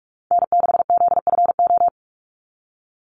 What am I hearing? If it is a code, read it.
Morse code, heard as a keyed tone, about 44 wpm, 700 Hz. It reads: N6ZFO
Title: Call Signs Repeated 3x - 45wpm